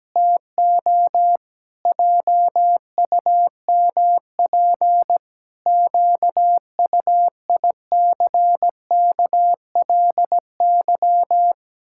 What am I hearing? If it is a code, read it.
Morse code, heard as a keyed tone, about 17 wpm, 700 Hz. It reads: TO JUMP QUICKLY
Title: Pangram Challenge v2 - 17wpm